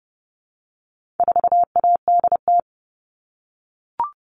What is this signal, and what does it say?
Morse code, heard as a keyed tone, about 30 wpm, 700 Hz. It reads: 4ADT
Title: All Cut Numbers - 4-Digit 30wpm